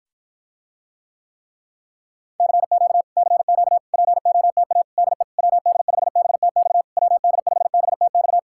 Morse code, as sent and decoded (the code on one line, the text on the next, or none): -.-. --.- / -.-. --.- / .--. --- - .- / -.. / .-- -.. ..... -... - -..- / .-- -.. ..... -... - -..-
CQ CQ POTA DE WD5BTX WD5BTX